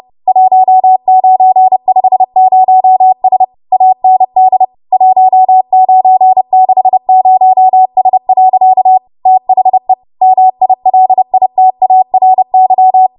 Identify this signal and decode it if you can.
Morse code, heard as a keyed tone, about 30 wpm, 750 Hz. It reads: 1950S AND 1960S. THE MILITARY
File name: Archive / 30 WPM / 30WPM